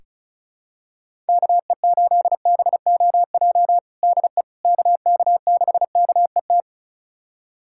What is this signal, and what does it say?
Morse code, heard as a keyed tone, about 36 wpm, 700 Hz. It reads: KE8BOJ DE KK6KET
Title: USA Callsign DE Callsign - 6 Characters - 35wpm